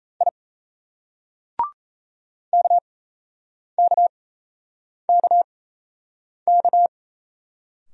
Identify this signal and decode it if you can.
Morse code, heard as a keyed tone, about 38 wpm, spaced as for 8 wpm, 700 Hz. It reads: I KKKK